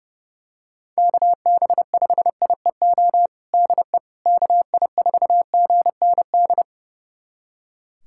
Morse code, heard as a keyed tone, about 30 wpm, 700 Hz. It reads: KB5IEO DE KI4GND
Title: USA Callsign DE Callsign 30wpm